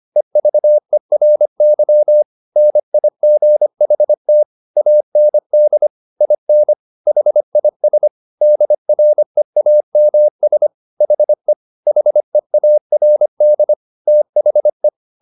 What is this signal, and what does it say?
Morse code, heard as a keyed tone, about 25 wpm, 600 Hz. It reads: EVERY NIGHT AND IN HIS DREAMS HE HEARD THE
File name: the old man and the sea part 1 25wpm